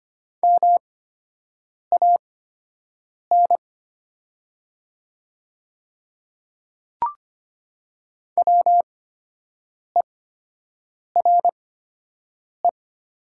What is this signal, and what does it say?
Morse code, heard as a keyed tone, about 25 wpm, spaced as for 7 wpm, 700 Hz. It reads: MAN WERE
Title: 8x Character Spacing - Top 200 Words - 25wpm